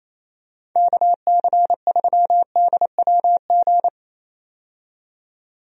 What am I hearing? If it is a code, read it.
Morse code, heard as a keyed tone, about 28 wpm, 700 Hz. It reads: KC3DWG
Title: USA Call Signs - 6 Characters - 28wpm